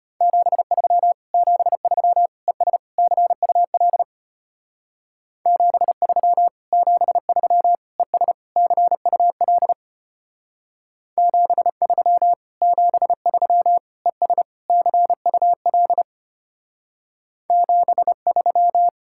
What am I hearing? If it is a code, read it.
Morse code, heard as a keyed tone, about 32 wpm, 700 Hz. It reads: 73 73 ES CUL 73 73 ES CUL 73 73 ES CUL 73